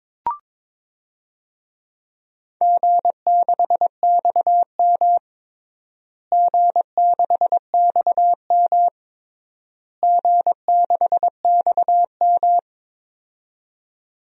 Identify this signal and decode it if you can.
Morse code, heard as a keyed tone, about 22 wpm, 700 Hz. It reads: G6XM G6XM G6XM
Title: DXCC-100 Call Signs - 4 Characters - Repeated 3x - 22wpm